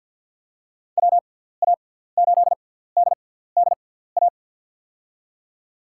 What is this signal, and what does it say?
Morse code, heard as a keyed tone, about 49 wpm, spaced as for 17 wpm, 700 Hz. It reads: WA8DDA
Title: USA Call Signs - 6x Character Spacing - 50wpm